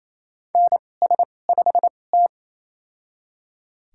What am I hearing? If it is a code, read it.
Morse code, heard as a keyed tone, about 28 wpm, spaced as for 20 wpm, 700 Hz. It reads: NS5T
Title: USA Call Signs - 2x Character Spacing - 28wpm